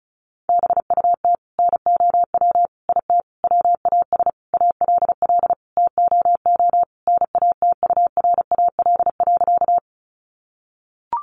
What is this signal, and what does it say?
Morse code, heard as a keyed tone, about 34 wpm, 700 Hz. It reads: BUT NOW IT WAS ALL TOO NATURAL.
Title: Sentences from Top 700 Words 35wpm